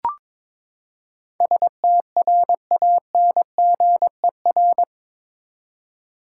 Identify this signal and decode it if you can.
Morse code, heard as a keyed tone, about 21 wpm, 700 Hz. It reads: STRANGER
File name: practice / 22wpm / Eight Letter Words - 22wpm